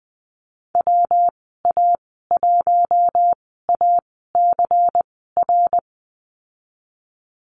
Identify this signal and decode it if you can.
Morse code, heard as a keyed tone, about 20 wpm, spaced as for 15 wpm, 700 Hz. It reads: WA1ACR
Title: 154 Lesson 25g - Letter p - Call Signs 20wpm